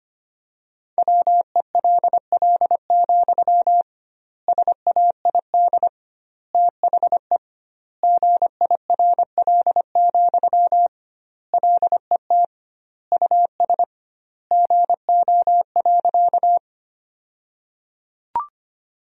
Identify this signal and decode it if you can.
Morse code, heard as a keyed tone, about 25 wpm, 700 Hz. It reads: WELL, SAID THE GIRL, LET US GO.
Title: 2x Word-Spacing - Sentences from Top 400 Words - 25wpm